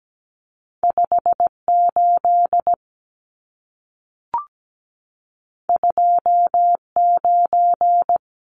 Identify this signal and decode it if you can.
Morse code, heard as a keyed tone, about 17 wpm, 700 Hz. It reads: 58 29
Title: Numbers - 2-Digit 17wpm